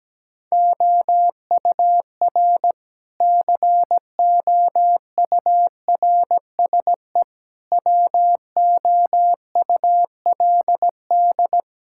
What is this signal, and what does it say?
Morse code, heard as a keyed tone, about 17 wpm, 700 Hz. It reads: OUR COURSE WOULD
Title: Sentences from Top 1000 Words - Review Part 6 - 17wpm